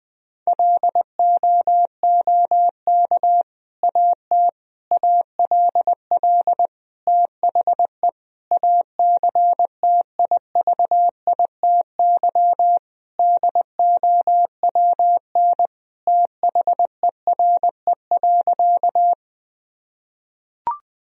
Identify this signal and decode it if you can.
Morse code, heard as a keyed tone, about 21 wpm, 700 Hz. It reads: LOOK AT ALL THE ACTIVITY DOWN THERE.
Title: Sentences from Top 1000 Words - Review Part 4 - 20wpm